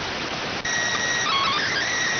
SSTV (slow-scan television), PD 290.